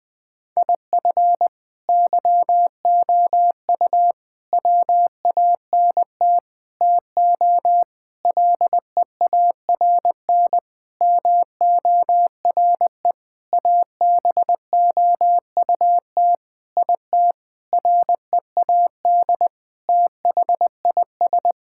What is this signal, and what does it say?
Morse code, heard as a keyed tone, about 20 wpm, 700 Hz. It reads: IF YOU WANT TO LEARN MORE ABOUT IT READ THIS
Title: Sentences with 7-Word Phrases - 20wpm